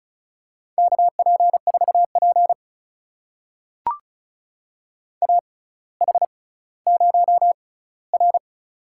Morse code, heard as a keyed tone, about 36 wpm, spaced as for 18 wpm, 700 Hz. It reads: KP4P AH0R
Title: USA Call Signs - 4 Characters - 6x Character Spacing - 35wpm